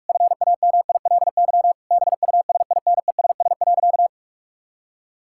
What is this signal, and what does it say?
Morse code, heard as a keyed tone, about 41 wpm, 700 Hz. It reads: FAMILY BUSINESS.
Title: Sentences from Top 1000 Words - Review Part 6 - 45wpm